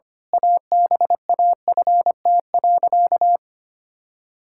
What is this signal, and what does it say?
Morse code, heard as a keyed tone, about 25 wpm, 700 Hz. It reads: ABAFT.
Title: Pangram Challenge v2 - 25wpm